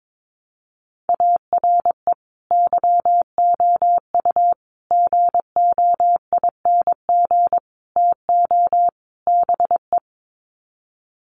Morse code, heard as a keyed tone, about 22 wpm, 700 Hz. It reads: ARE YOU GOING TO BE
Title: Sets of 5 Words Encore 22wpm